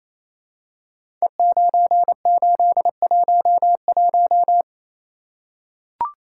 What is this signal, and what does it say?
Morse code, heard as a keyed tone, about 28 wpm, 700 Hz. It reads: E9811
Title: Code Groups - 5 Letter:Numbers - 28wpm